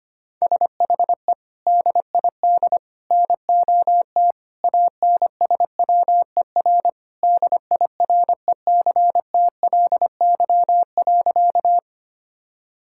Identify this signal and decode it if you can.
Morse code, heard as a keyed tone, about 25 wpm, 700 Hz. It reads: SHE DID NOT ANSWER DIRECTLY.
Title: Sentences from Top 1000 Words - Review Part 5 - 25wpm